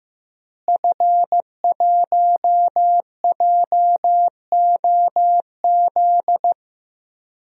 Morse code, heard as a keyed tone, about 15 wpm, 700 Hz. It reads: F1JOZ
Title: DXCC-100 Call Signs - 5 Characters - Repeated 3x - 15wpm